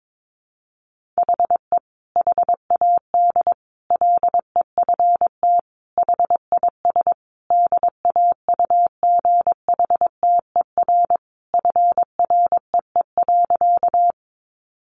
Morse code, heard as a keyed tone, about 22 wpm, 700 Hz. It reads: HE HAD LEFT HIS DAUGHTER FREE.
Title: Sentences from Top 1500 Words 22wpm